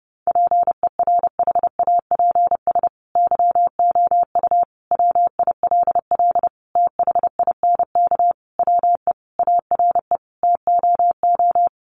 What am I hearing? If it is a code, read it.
Morse code, heard as a keyed tone, about 30 wpm, 700 Hz. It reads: PERHAPS YOU WILL THINK WE ARE TOO
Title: Sentences from Top 1400 Words 30wpm